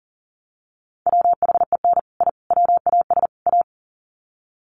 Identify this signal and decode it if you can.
Morse code, heard as a keyed tone, about 37 wpm, 700 Hz. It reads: WHEN I WAS A
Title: Sets of 4 Words 40wpm